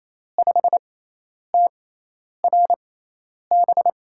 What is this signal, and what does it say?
Morse code, heard as a keyed tone, about 28 wpm, spaced as for 10 wpm, 700 Hz. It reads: 5TRB
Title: USA Call Signs - 6 Characters - 6x Character Spacing - 28wpm